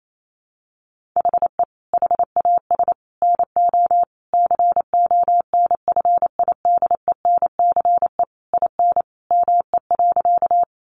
Morse code, heard as a keyed tone, about 28 wpm, 700 Hz. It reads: HE HAS NO CONFIDENCE IN ME.
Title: Sentences from Top 1500 Words 28wpm